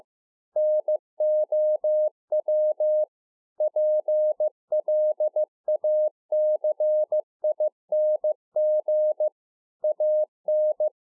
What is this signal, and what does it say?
Morse code, heard as a keyed tone, about 15 wpm, 610 Hz. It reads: NOW PLACING AN